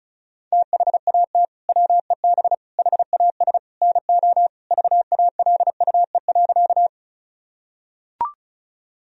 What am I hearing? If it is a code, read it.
Morse code, heard as a keyed tone, about 34 wpm, 700 Hz. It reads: THAT WEB HAS NO VALUE.
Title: Sentences from Top 1000 Words - Review Part 4 - 35wpm